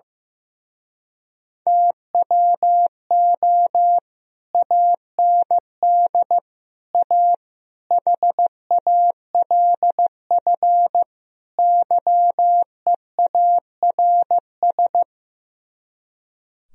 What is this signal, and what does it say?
Morse code, heard as a keyed tone, about 15 wpm, 700 Hz. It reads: TWO AND A HALF YEARS
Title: Speed-Racing - Sets of 5 Words 15wpm